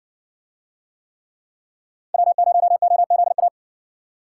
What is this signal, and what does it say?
Morse code, heard as a keyed tone, about 54 wpm, 700 Hz. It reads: W0OZA